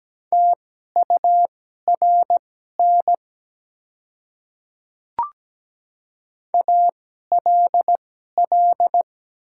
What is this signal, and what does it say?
Morse code, heard as a keyed tone, about 17 wpm, spaced as for 12 wpm, 700 Hz. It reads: TURN ALL
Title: 2x Character Spacing - Top 200 Words - 17wpm